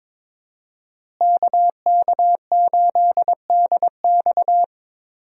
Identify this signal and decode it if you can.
Morse code, heard as a keyed tone, about 23 wpm, 700 Hz. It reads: KK8DX
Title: USA Call Signs - 5 Characters - Repeated 2x - 22wpm